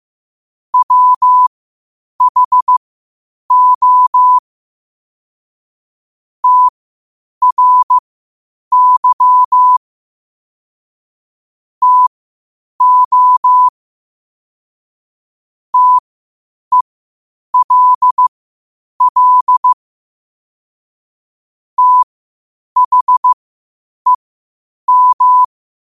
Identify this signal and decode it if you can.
Morse code, heard as a keyed tone, about 15 wpm, spaced as for 8 wpm, 1000 Hz. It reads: WHO TRY TO TELL THEM
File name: qotdfiles / 8wpm